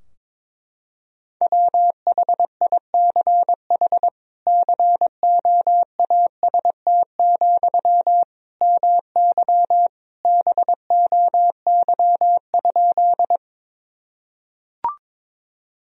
Morse code, heard as a keyed tone, about 22 wpm, 700 Hz. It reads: WHICH COAST, MY BOY?
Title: Sentences from Top 1000 Words - Review Part 4 - 22wpm